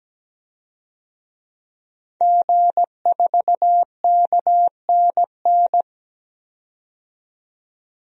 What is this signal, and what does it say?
Morse code, heard as a keyed tone, about 17 wpm, 700 Hz. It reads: G4KNN
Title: DXCC-100 Call Signs - 5 Characters - 17wpm